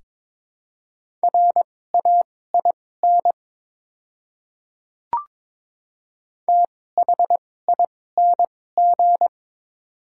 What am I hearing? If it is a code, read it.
Morse code, heard as a keyed tone, about 22 wpm, spaced as for 16 wpm, 700 Hz. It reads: RAIN THING
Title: 2x Character Spacing - Top 300 Words - 22wpm